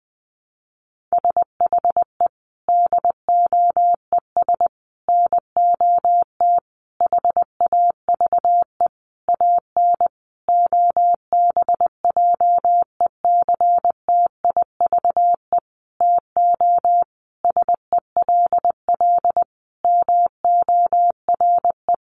Morse code, heard as a keyed tone, about 20 wpm, 700 Hz. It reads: SHE DOES NOT HAVE AN OBJECTIVE TO SELL MORE
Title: Sentences from Top 1200 Words 20wpm